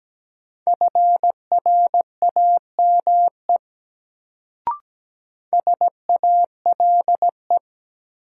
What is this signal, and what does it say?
Morse code, heard as a keyed tone, about 17 wpm, 700 Hz. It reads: FRAME SALE
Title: Sentences from Top 1400 Words - New Word Drill Alt - 17wpm